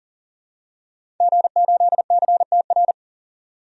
Morse code, heard as a keyed tone, about 40 wpm, 700 Hz. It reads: G8CTR
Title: DXCC-100 Call Signs - 5 Characters - 40wpm